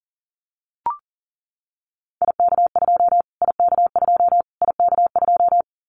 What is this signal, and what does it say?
Morse code, heard as a keyed tone, about 40 wpm, 700 Hz. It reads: IK2 IK2 IK2
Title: Callsign Prefixes for Top 25 DXCC - Repeated 3x - 40wpm